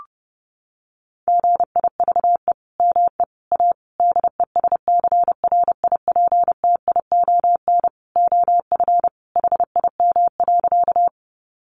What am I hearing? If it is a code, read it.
Morse code, heard as a keyed tone, about 30 wpm, 700 Hz. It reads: GIVE ME A DESCRIPTION OF HIM.
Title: Sentences from Top 1300 Words 30wpm